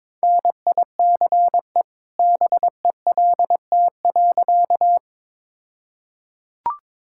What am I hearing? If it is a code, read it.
Morse code, heard as a keyed tone, about 22 wpm, 690 Hz. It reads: NICE BELT.
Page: morse-code-ninja-practice-22wpm